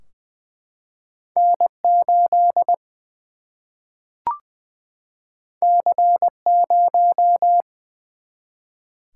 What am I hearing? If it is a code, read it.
Morse code, heard as a keyed tone, about 21 wpm, 700 Hz. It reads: N 8 C 0
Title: Single-Letter Single-Number 20wpm